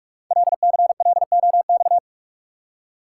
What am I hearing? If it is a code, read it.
Morse code, heard as a keyed tone, about 44 wpm, 700 Hz. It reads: LCLOX
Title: Code Groups - 5 Letter:Numbers - 45wpm